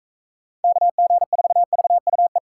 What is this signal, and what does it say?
Morse code, heard as a keyed tone, about 41 wpm, 700 Hz. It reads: KG4VUE